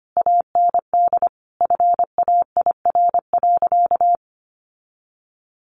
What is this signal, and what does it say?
Morse code, heard as a keyed tone, about 25 wpm, 700 Hz. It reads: AND FAIR.
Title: Sentences from Top 1300 Words 25wpm